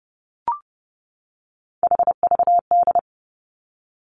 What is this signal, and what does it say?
Morse code, heard as a keyed tone, about 30 wpm, 700 Hz. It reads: HVD
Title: Three Challenging Letter-Number Combinations 30wpm